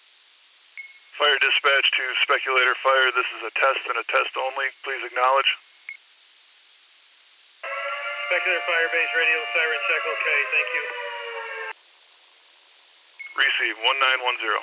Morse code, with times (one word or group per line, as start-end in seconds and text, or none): none